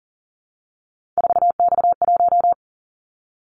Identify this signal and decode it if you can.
Morse code, heard as a keyed tone, about 40 wpm, 700 Hz. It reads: 4X1
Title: Callsign Prefixes for Top 50 DXCC - 40wpm